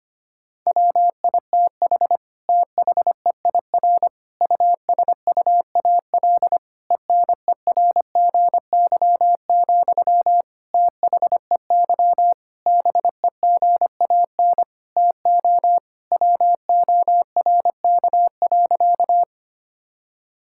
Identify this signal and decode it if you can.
Morse code, heard as a keyed tone, about 25 wpm, 700 Hz. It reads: WITH THEIR USUAL ENERGY, THEY BEGAN TO WORK.
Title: Sentences from Top 1000 Words - Review Part 6 - 25wpm